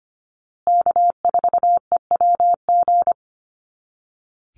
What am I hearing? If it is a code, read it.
Morse code, heard as a keyed tone, about 25 wpm, 700 Hz. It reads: K4EWG
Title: Call Signs Repeated 2x - 25wpm